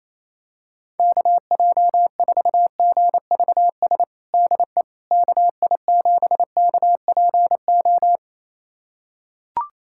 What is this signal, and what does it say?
Morse code, heard as a keyed tone, about 28 wpm, 700 Hz. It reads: KJ4GVS DE KI7KPO
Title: USA Callsign DE Callsign - 6 Characters - 28wpm